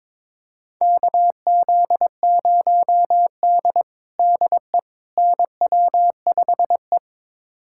Morse code, heard as a keyed tone, about 22 wpm, 700 Hz. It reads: KZ0D DE NW5E
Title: USA Callsign DE Callsign - 4 Characters - 22wpm